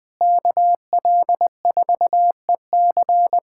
Morse code, heard as a keyed tone, about 20 wpm, 700 Hz. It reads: KL4EC
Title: USA Callsign DE Callsign - 5 Characters - 20wpm